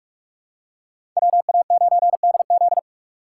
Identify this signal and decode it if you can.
Morse code, heard as a keyed tone, about 48 wpm, 700 Hz. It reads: WA9DZ